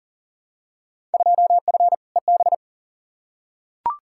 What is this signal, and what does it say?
Morse code, heard as a keyed tone, about 40 wpm, 700 Hz. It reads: 2F EB
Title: Field Day Exchanges Encore - 40wpm